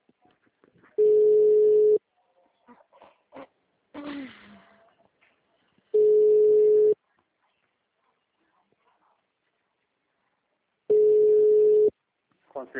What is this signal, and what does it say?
Ringback tone (Europe, 425 Hz).
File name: out- -2022- - - .43392